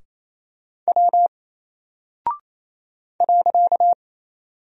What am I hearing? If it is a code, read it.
Morse code, heard as a keyed tone, about 27 wpm, 700 Hz. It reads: W .
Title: Single Character (Letter:Number:Punctuation:Prosign) Alt 28wpm